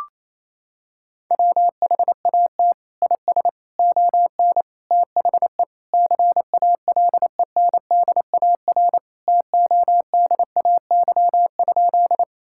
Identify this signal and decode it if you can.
Morse code, heard as a keyed tone, about 28 wpm, 700 Hz. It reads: WHAT IS ON THE CALENDAR TODAY?
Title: Sentences from Top 1600 Words 28wpm